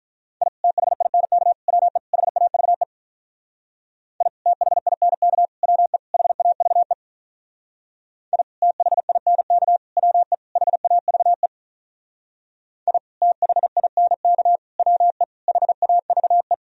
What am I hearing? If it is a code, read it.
Morse code, heard as a keyed tone, about 44 wpm, 700 Hz. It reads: I THINK WE HAVE I THINK WE HAVE I THINK WE HAVE I THINK WE HAVE
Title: Speed-Racing - Sets of 4 Words 35wpm